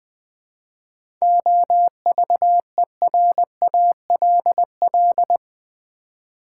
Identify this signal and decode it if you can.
Morse code, heard as a keyed tone, about 20 wpm, 700 Hz. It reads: OVERALL
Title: Seven Letter Words - 20wpm